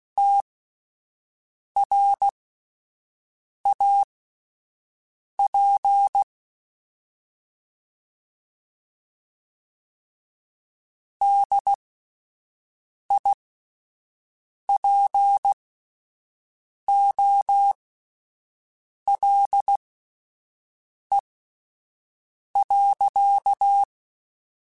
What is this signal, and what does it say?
Morse code, heard as a keyed tone, about 16 wpm, spaced as for 5 wpm, 800 Hz. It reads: TRAP DIPOLE.